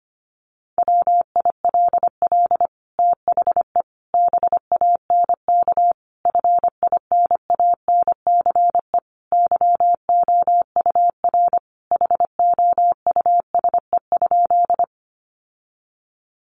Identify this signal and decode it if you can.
Morse code, heard as a keyed tone, about 25 wpm, 700 Hz. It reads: WILL THE BANK FINANCE YOUR HOUSE?
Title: Sentences from Top 1500 Words 25wpm